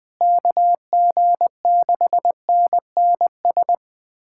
Morse code, heard as a keyed tone, about 20 wpm, 700 Hz. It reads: KG6NNS
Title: USA Callsign DE Callsign 20wpm